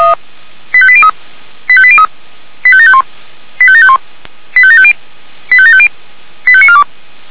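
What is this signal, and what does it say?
Five-tone selective call (ZVEI), heard as quick runs of five tones, tones of about 70 ms. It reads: 85803 85803 85751 85751 85750 85750 85942
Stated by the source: Five-tone ZVEI selective call received by SDR, from the fire brigade Feuerwehr Orlamünde.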